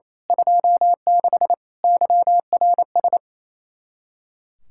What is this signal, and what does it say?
Morse code, heard as a keyed tone, about 27 wpm, 700 Hz. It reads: 26 YRS
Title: Speed-Racing Common QSO Phrases - 28wpm